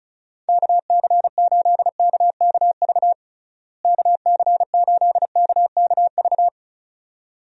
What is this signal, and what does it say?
Morse code, heard as a keyed tone, about 34 wpm, 700 Hz. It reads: KC8KKV KC8KKV
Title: USA Call Signs - 6 Characters - Repeated 2x - 35wpm